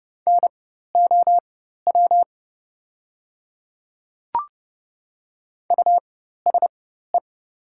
Morse code, heard as a keyed tone, about 30 wpm, spaced as for 14 wpm, 700 Hz. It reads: NOW USE
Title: 4x Character Spacing - Top 100 Words - 30wpm